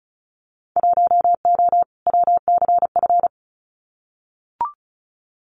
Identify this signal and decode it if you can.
Morse code, heard as a keyed tone, about 34 wpm, 690 Hz. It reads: 1O WCF
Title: Winter Field Day Exchanges 35wpm